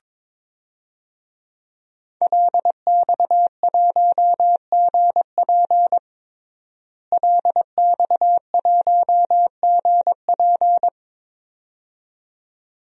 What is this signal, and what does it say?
Morse code, heard as a keyed tone, about 22 wpm, 700 Hz. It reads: LX1GP LX1GP
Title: DXCC-100 Call Signs - 5 Characters - Repeated 2x - 22wpm